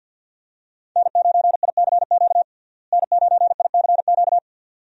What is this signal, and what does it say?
Morse code, heard as a keyed tone, about 51 wpm, 700 Hz. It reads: N9ICQ N9ICQ
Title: USA Call Signs - 5 Characters - Repeated 2x - 50wpm